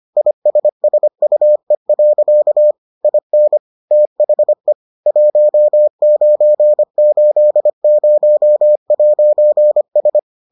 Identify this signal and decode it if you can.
Morse code, heard as a keyed tone, about 25 wpm, 600 Hz. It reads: ISSUE. IN THE 1980'S